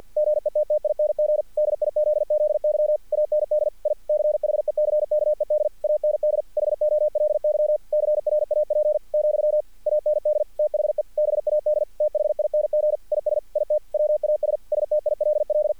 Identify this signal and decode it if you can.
Morse code, heard as a keyed tone, about 50 wpm, 600 Hz, with radio noise. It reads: GETTING DIZZY AND I CHECKED AND HOLY CRAP , AND THE BAD THING IS IT WAS STILL
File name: archived / CRSnet